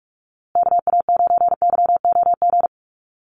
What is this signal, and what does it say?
Morse code, heard as a keyed tone, about 44 wpm, 700 Hz. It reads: KA9YOG
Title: USA Call Signs Repeated 2x 45wpm